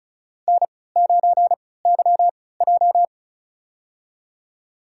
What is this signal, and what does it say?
Morse code, heard as a keyed tone, about 36 wpm, spaced as for 20 wpm, 700 Hz. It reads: N9YJ
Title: USA Call Signs - 4 Characters - 3x Character Spacing - 35wpm